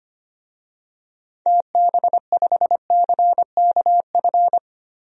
Morse code, heard as a keyed tone, about 25 wpm, 700 Hz. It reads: TB5CKF
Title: DXCC-100 Call Signs - 6 Characters - Repeated 2x - 25wpm